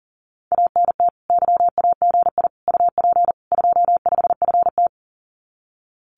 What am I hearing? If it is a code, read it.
Morse code, heard as a keyed tone, about 41 wpm, 700 Hz. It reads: ANT YAGI UP 25FT
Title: Common QSO Phrases Encore 40wpm